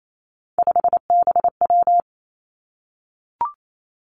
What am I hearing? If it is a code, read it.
Morse code, heard as a keyed tone, about 28 wpm, 700 Hz. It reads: 5BW
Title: Three Challenging Letter-Number Combinations 28wpm